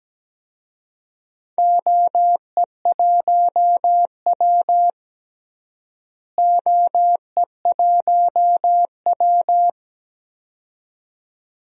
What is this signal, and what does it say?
Morse code, heard as a keyed tone, about 17 wpm, 700 Hz. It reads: OE1W OE1W
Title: DXCC-100 Call Signs - 4 Characters - Repeated 2x - 17wpm